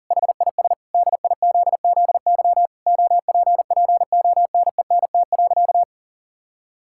Morse code, heard as a keyed tone, about 40 wpm, 700 Hz. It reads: HIS DIZZY OPPONENT.
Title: Pangram Challenge v2 - 40wpm